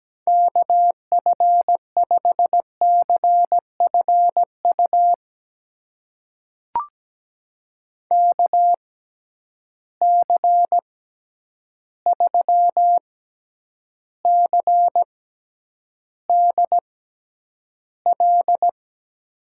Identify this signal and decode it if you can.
Morse code, heard as a keyed tone, about 17 wpm, spaced as for 9 wpm, 700 Hz. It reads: KF5CFU KC3CDL
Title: USA Call Signs - 6 Characters - 6x Character Spacing - 17wpm